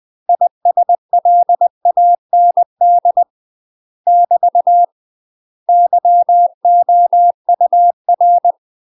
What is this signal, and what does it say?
Morse code, heard as a keyed tone, about 20 wpm, 700 Hz. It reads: ISLAND <BT> YOUR
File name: UK MASTER QSOs TRAINING / 20WPM / UK MASTER QSOs 0032